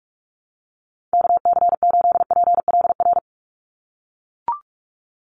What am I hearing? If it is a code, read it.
Morse code, heard as a keyed tone, about 44 wpm, 700 Hz. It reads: KC8PLR
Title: USA Call Signs 45wpm